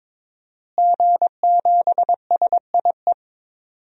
Morse code, heard as a keyed tone, about 22 wpm, 700 Hz. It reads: G7SIE
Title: DXCC-100 Call Signs - 5 Characters - Repeated 2x - 22wpm